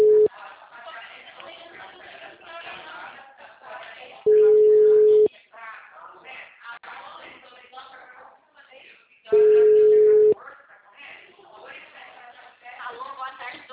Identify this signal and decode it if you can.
Ringback tone (Europe, 425 Hz), with speech.